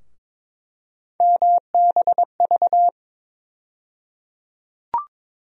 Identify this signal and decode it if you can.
Morse code, heard as a keyed tone, about 21 wpm, 700 Hz. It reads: MBV
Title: USA Call Sign Postfixes - 3 Letters 22wpm